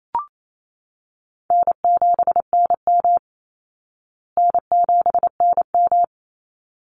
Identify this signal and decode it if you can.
Morse code, heard as a keyed tone, about 28 wpm, 700 Hz. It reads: N7NM N7NM
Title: Call Signs Repeated 3x - 28wpm